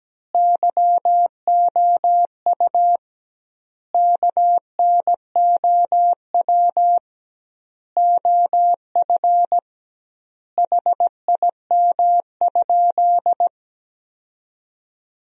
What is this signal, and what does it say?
Morse code, heard as a keyed tone, about 17 wpm, 700 Hz. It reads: YOU KNOW OF HIM?
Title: 2x Word-Spacing - Sentences from Top 100 Words - 17wpm